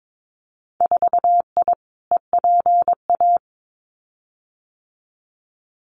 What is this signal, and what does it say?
Morse code, heard as a keyed tone, about 22 wpm, 700 Hz. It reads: 4I EPA
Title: Winter Field Day Exchanges Encore 22wpm